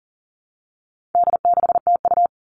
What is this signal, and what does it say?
Morse code, heard as a keyed tone, about 40 wpm, 700 Hz. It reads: D6TU
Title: All Cut Numbers - 4-Digit 40wpm